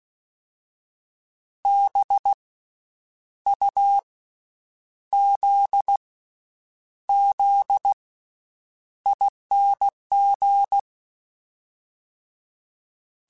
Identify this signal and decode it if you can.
Morse code, heard as a keyed tone, about 16 wpm, 800 Hz. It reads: B U Z Z ING
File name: Special INGs-mp3s / Special INGs-mp3s / Slow ing @15